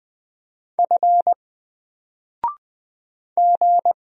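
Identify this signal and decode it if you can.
Morse code, heard as a keyed tone, about 20 wpm, 700 Hz. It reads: F G
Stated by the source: Single letter and number Morse drill at 20 wpm, from N4LCD.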